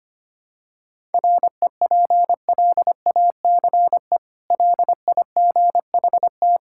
Morse code, heard as a keyed tone, about 25 wpm, 700 Hz. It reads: REPLACE LIGHT
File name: Easy Sentences with Hints - 25wpm